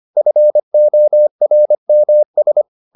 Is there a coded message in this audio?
Yes: Morse code from 0.2 to 2.6 s, heard as a keyed tone, about 25 wpm, 600 Hz. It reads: FORMS